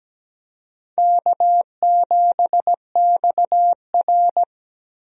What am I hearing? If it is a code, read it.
Morse code, heard as a keyed tone, about 17 wpm, 700 Hz. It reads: K7XR
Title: USA Call Signs - 4 Characters - Repeated 2x - 17wpm